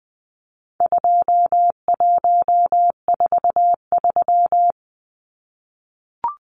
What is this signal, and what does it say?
Morse code, heard as a keyed tone, about 20 wpm, 700 Hz. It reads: 2143
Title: Numbers - 4-Digit 20wpm